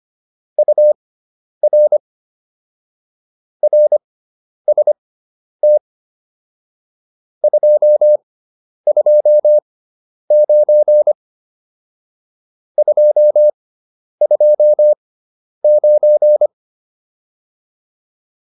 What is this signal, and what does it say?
Morse code, heard as a keyed tone, about 25 wpm, spaced as for 10 wpm, 600 Hz. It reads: UR RST 229 229
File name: QSO407 10